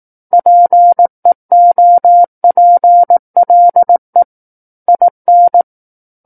Morse code, heard as a keyed tone, about 18 wpm, 700 Hz. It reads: PEOPLE IN
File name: 18 WPM